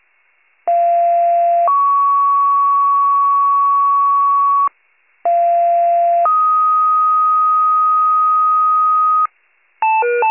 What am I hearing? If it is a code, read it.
Two-tone page, heard as a one-second tone, then a three-second tone, about 690 then 1120 Hz, about 690 then 1250 Hz. It is heard from 0.7 to 9.3 s.
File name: pager / 41 FIRE / wcalert